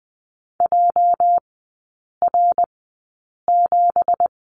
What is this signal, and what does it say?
Morse code, heard as a keyed tone, about 20 wpm, spaced as for 8 wpm, 690 Hz. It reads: JR7
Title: 212 Lesson 34e - Number 8 - Three Characters 20wpm